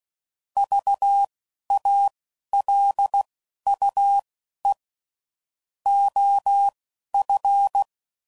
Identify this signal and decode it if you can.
Morse code, heard as a keyed tone, about 16 wpm, spaced as for 11 wpm, 800 Hz. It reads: VALUE OF
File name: CW Code Groups @10WPM / Technical Subjects / XL